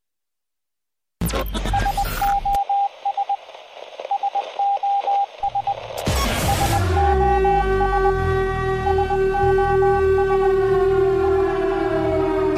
Morse code, heard as a keyed tone, about 20 wpm, 800 Hz, with radio noise. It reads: SOS SOS SOS SOS SO